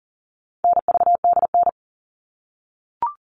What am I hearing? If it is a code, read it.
Morse code, heard as a keyed tone, about 40 wpm, 700 Hz. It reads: NVDN
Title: All Cut Numbers - 4-Digit 40wpm